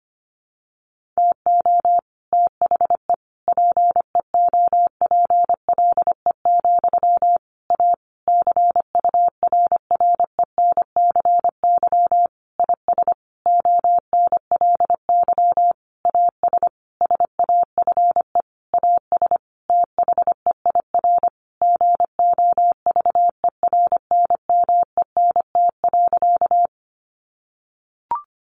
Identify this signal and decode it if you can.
Morse code, heard as a keyed tone, about 25 wpm, 700 Hz. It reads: TO THE PEOPLE, A CURRENCY IS ONLY AS SAFE AS THEIR GOVERNMENT.
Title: Sentences from Top 1500 Words 25wpm